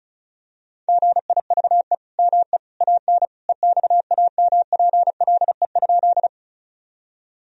Morse code, heard as a keyed tone, about 34 wpm, 700 Hz. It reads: GIVE ME AN EXAMPLE?
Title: Sentences from Top 1000 Words - Review Part 1 - 35wpm